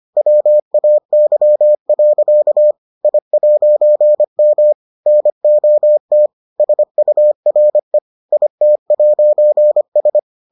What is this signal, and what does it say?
Morse code, heard as a keyed tone, about 25 wpm, 600 Hz. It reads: WAY. I'M NOT SURE IT'S